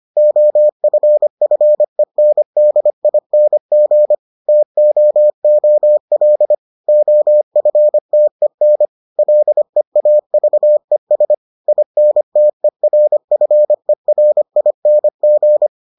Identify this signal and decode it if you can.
Morse code, heard as a keyed tone, about 25 wpm, 600 Hz. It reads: OFFENDING TOOL OFTEN LEAVES INTERFERING